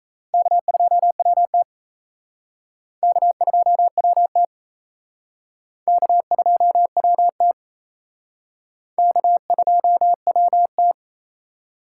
Morse code, heard as a keyed tone, about 34 wpm, 700 Hz. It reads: K2WT K2WT K2WT K2WT
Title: Speed Racing - USA 1x2 Call Signs 28wpm